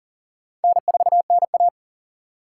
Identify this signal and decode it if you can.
Morse code, heard as a keyed tone, about 42 wpm, 690 Hz. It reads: N4NA
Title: USA Call Signs - 4 Characters - 2x Character Spacing - 40wpm